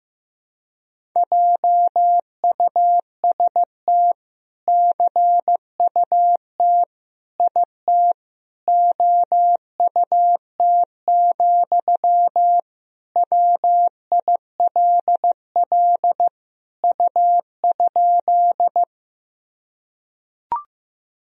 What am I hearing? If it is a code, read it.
Morse code, heard as a keyed tone, about 15 wpm, 700 Hz. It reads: JUST CUT IT OUT, WILL U?
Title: Sentences from Top 300 Words and Abbreviations 15wpm